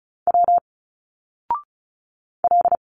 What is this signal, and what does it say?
Morse code, heard as a keyed tone, about 34 wpm, 700 Hz. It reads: W L
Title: Single Letters 35wpm